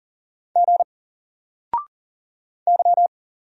Morse code, heard as a keyed tone, about 40 wpm, 700 Hz. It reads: G Y
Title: Single Character (Letter:Number:Punctuation:Prosign) Alt 40wpm